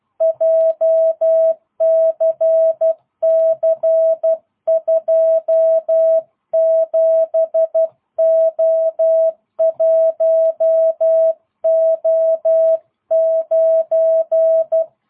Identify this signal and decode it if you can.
Morse code, heard as a keyed tone, about 12 wpm, 650 Hz. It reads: JCC27O1O9
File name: jh8cbh / 2022cw